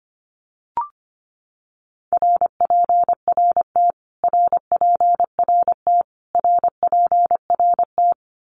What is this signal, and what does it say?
Morse code, heard as a keyed tone, about 25 wpm, 700 Hz. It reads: RPRT RPRT RPRT
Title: Abbreviations Repeated 3x - 25wpm